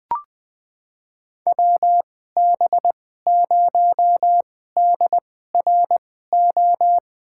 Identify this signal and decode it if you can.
Morse code, heard as a keyed tone, about 20 wpm, spaced as for 14 wpm, 700 Hz. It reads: WB0DRO
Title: USA Call Signs - 2x Character Spacing - 20wpm